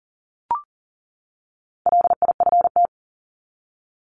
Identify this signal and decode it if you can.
Morse code, heard as a keyed tone, about 40 wpm, 700 Hz. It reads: LIFT
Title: Top 751-1000 Words 40wpm